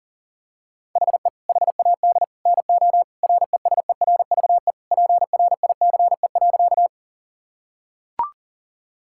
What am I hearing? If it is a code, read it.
Morse code, heard as a keyed tone, about 40 wpm, 700 Hz. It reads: HE HAD NO RESERVE PRICE.
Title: Sentences from Top 1000 Words - Review Part 6 - 40wpm